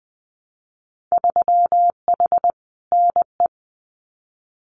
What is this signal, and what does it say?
Morse code, heard as a keyed tone, about 20 wpm, 700 Hz. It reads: 3H NE
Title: Winter Field Day Exchanges Encore 20wpm